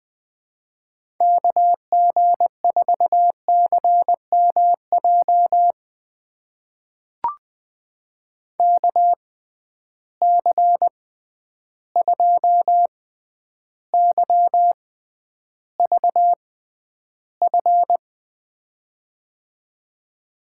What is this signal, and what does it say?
Morse code, heard as a keyed tone, about 20 wpm, spaced as for 10 wpm, 700 Hz. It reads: KG4CMJ KC2YVF
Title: USA Call Signs - 6 Characters - 6x Character Spacing - 20wpm